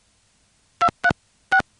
DTMF, heard as telephone touch tones, tones of about 70 ms. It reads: 236